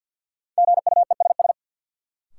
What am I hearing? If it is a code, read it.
Morse code, heard as a keyed tone, about 47 wpm, 700 Hz. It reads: GUESS
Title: Sentences from Top 1500 Words - New Word Drill - 50wpm